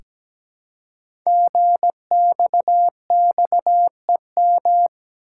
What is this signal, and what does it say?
Morse code, heard as a keyed tone, about 17 wpm, 700 Hz. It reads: GXXEM
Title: Code Groups - 5 Letters - 17wpm